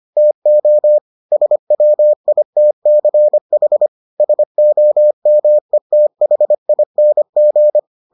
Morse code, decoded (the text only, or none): TO SWITCH SOMETHING